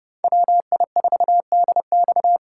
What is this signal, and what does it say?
Morse code, heard as a keyed tone, about 37 wpm, 700 Hz. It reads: WI4DX
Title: Speed Racing - USA Call Signs 30wpm